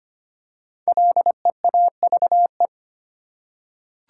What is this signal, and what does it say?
Morse code, heard as a keyed tone, about 25 wpm, 700 Hz. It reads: LEAVE
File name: 5 Letter Words - First 3 Letters Given - 25wpm